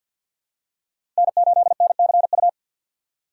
Morse code, heard as a keyed tone, about 47 wpm, 700 Hz. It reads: N8NCU
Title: USA Call Signs - 3x Character Spacing - 50wpm